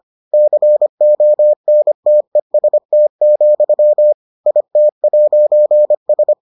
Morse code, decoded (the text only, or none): CONTEST, IT'S